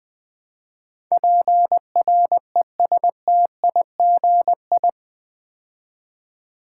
Morse code, heard as a keyed tone, about 20 wpm, 700 Hz. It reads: PRESTIGI
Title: Build A Word - 4 Letters At A Time - 11-Letter Words - 20wpm